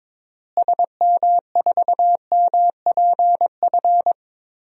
Morse code, heard as a keyed tone, about 22 wpm, 700 Hz. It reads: SM4MPF
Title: DXCC-100 Call Signs - 6 Characters - Repeated 2x - 22wpm